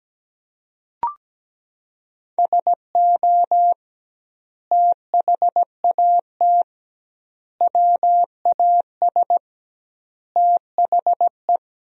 Morse code, heard as a keyed tone, about 17 wpm, 700 Hz. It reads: SO THAT WAS THE
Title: 2x Word-Spacing - Sentences from Top 100 Words - 17wpm